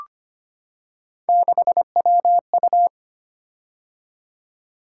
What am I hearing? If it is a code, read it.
Morse code, heard as a keyed tone, about 25 wpm, 700 Hz. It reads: 6WU
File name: Three Letter-Number Combinations - 25wpm